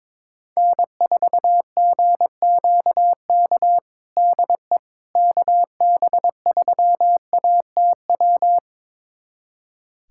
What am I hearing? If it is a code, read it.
Morse code, heard as a keyed tone, about 22 wpm, 700 Hz. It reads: N4GQK DE KB3ATW
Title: USA Callsign DE Callsign 22wpm